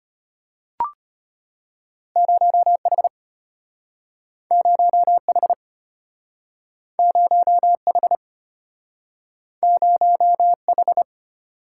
Morse code, heard as a keyed tone, about 34 wpm, 700 Hz. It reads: 0H 0H 0H 0H